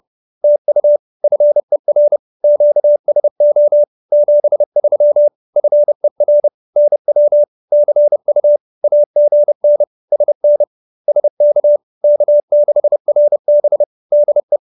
Morse code, heard as a keyed tone, about 30 wpm, 590 Hz. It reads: TU FER QSO 73 FER NW CU AGN SN SK K6RB DE